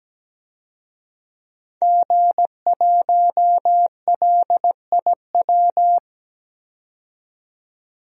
Morse code, heard as a keyed tone, about 17 wpm, 700 Hz. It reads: G1LIW
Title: DXCC-100 Call Signs - 5 Characters - 17wpm